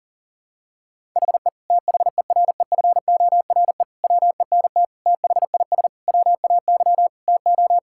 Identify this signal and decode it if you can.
Morse code, heard as a keyed tone, about 37 wpm, 690 Hz. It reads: HE THEREFORE WENT THIS WAY TO
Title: Sentences from Top 1000 Words - Review Part 5 - 40wpm